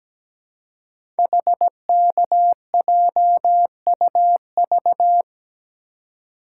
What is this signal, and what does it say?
Morse code, heard as a keyed tone, about 17 wpm, 700 Hz. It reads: HKJUV
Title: Code Groups - 5 Letters - 17wpm